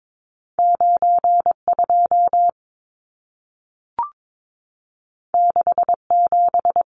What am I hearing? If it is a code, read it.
Morse code, heard as a keyed tone, about 22 wpm, 700 Hz. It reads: 92 67
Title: Numbers - 2-Digit 22wpm